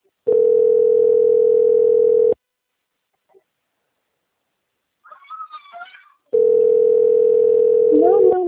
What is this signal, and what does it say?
Ringback tone (North America), 440 + 480 Hz, with speech.
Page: Maringa - PR